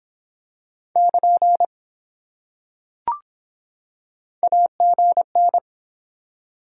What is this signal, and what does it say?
Morse code, heard as a keyed tone, about 23 wpm, 700 Hz. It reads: <KN> AGN